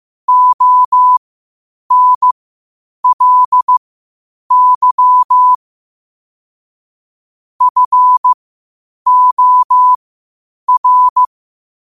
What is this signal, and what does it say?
Morse code, heard as a keyed tone, about 15 wpm, spaced as for 8 wpm, 1000 Hz. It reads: ONLY FOR
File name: 8wpm